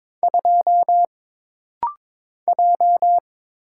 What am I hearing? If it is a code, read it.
Morse code, heard as a keyed tone, about 22 wpm, 700 Hz. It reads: 2 J
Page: This rapid-fire timing is geared to push your ICR (Instant Character Recognition) skills.